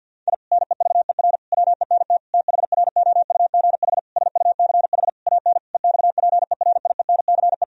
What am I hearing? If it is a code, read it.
Morse code, heard as a keyed tone, about 50 wpm, 700 Hz. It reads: I NEVER WENT THROUGH SUCH AN EXPERIENCE
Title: Sentences from Top 1000 Words - Review Part 3 - 50wpm